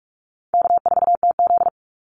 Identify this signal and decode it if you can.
Morse code, heard as a keyed tone, about 44 wpm, 700 Hz. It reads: K4TZ
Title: Call Signs Repeated 3x - 45wpm